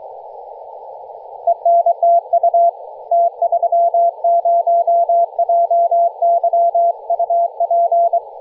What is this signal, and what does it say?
Morse code, heard as a keyed tone, about 24 wpm, 680 Hz. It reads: ENTU T30JY UP
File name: posts / 2016-nov-13-0348z-t30jy-28-034mhz-cw